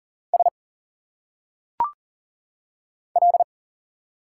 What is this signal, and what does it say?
Morse code, heard as a keyed tone, about 40 wpm, 700 Hz. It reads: S L
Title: Single Letters - Phonetic Pronunciation 40wpm